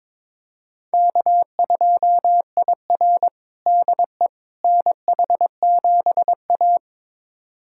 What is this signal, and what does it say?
Morse code, heard as a keyed tone, about 22 wpm, 700 Hz. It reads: K2IR DE NH7A
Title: USA Callsign DE Callsign - 4 Characters - 22wpm